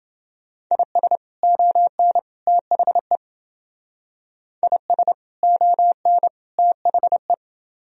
Morse code, heard as a keyed tone, about 28 wpm, 700 Hz. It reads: IS ON THE IS ON THE